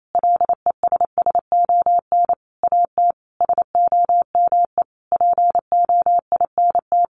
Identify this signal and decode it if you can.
Morse code, heard as a keyed tone, about 28 wpm, 700 Hz. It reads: LESSON AT SOME POINT
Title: Sentences from Top 1200 Words 28wpm